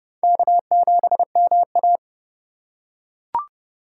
Morse code, heard as a keyed tone, about 25 wpm, 700 Hz. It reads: K7MA
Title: Speed Racing - USA Call Signs 20wpm